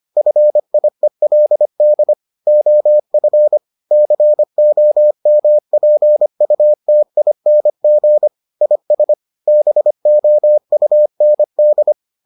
Morse code, heard as a keyed tone, about 25 wpm, 600 Hz. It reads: FIELD OF COMPUTING IS BOUND